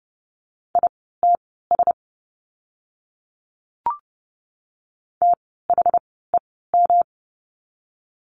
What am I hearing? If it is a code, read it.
Morse code, heard as a keyed tone, about 30 wpm, spaced as for 17 wpm, 700 Hz. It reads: ITS THEM
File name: course / 30:10wpm / 115 Lesson 19f - Letter w - Words 30wpm